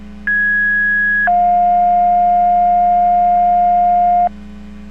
Two-tone page, heard as a one-second tone, then a three-second tone, about 1650 then 710 Hz. It is heard from 0.3 to 4.3 s.